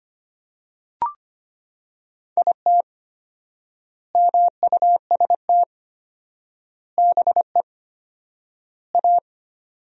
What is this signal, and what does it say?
Morse code, heard as a keyed tone, about 25 wpm, 700 Hz. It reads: IT MUST BE A
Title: 4x Word-Spacing - Sentences from Top 500 Words - 25wpm